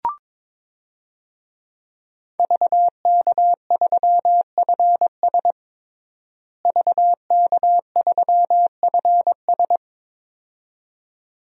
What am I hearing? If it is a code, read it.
Morse code, heard as a keyed tone, about 22 wpm, 700 Hz. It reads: VK3FS VK3FS
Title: USA Call Signs - 5 Characters - Repeated 2x - 22wpm